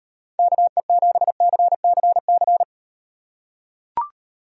Morse code, heard as a keyed tone, about 38 wpm, 700 Hz. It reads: KE7CCC